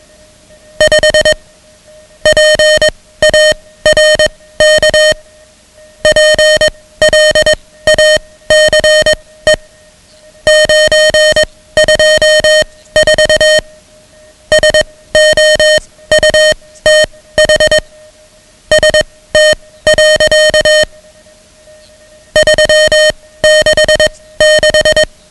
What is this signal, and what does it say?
Morse code, heard as a keyed tone, about 21 wpm, spaced as for 16 wpm, 620 Hz, with noise. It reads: 5 PARK PLACE 924 SOUTH ST. 366
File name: ham radio / CodeQuick / Code Quick 10